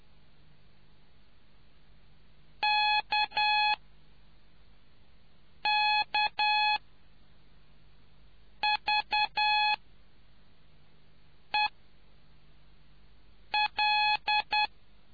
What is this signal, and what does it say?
Morse code, heard as a keyed tone, about 10 wpm, spaced as for 4 wpm, 3400 Hz. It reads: KKVEL